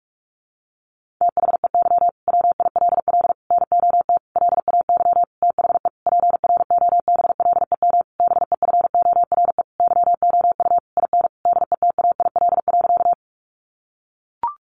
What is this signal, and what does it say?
Morse code, heard as a keyed tone, about 44 wpm, 700 Hz. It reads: THEY WILL NOT LAY THE PROBLEM BEFORE YOU IN DETAIL.
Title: Sentences from Top 1500 Words 45wpm